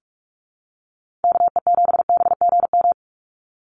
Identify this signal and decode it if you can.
Morse code, heard as a keyed tone, about 44 wpm, 700 Hz. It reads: KE7BGM
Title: NA SOTA Activator Callsigns 45wpm